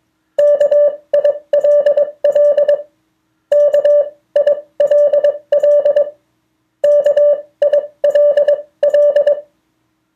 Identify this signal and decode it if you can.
Morse code, heard as a keyed tone, about 21 wpm, spaced as for 18 wpm, 580 Hz. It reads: KILL KILL KILL
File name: 2of4-words-1